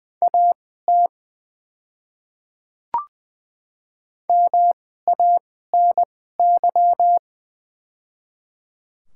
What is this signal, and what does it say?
Morse code, heard as a keyed tone, about 20 wpm, spaced as for 14 wpm, 700 Hz. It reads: AT MANY
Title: 2x Character Spacing - Top 100 Words - 20wpm